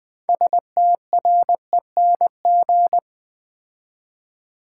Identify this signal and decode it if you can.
Morse code, heard as a keyed tone, about 20 wpm, 700 Hz. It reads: STRENG
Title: Build A Word - 3 Letters At A Time - 8-Letter Words - 20wpm